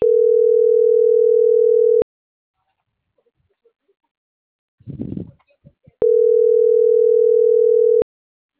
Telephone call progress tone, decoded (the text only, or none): ringback tone